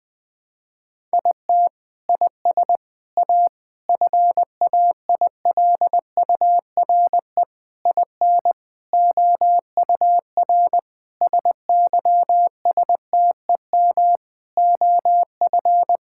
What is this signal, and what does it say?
Morse code, heard as a keyed tone, about 21 wpm, 700 Hz. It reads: IT IS A FAILURE IN OUR SYSTEM OF
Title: Sentences from Top 1000 Words - Review Part 6 - 20wpm